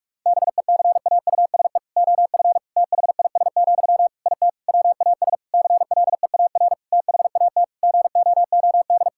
Morse code, heard as a keyed tone, about 43 wpm, 700 Hz. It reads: BECAUSE OF THIS, IT WAS CLEAR THAT GOOD
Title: Sentences from Top 1000 Words - Review Part 6 - 45wpm